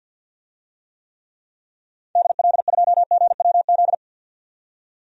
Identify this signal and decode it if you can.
Morse code, heard as a keyed tone, about 51 wpm, 700 Hz. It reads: DL2GWZ
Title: DXCC-100 Call Signs - 6 Characters - 50wpm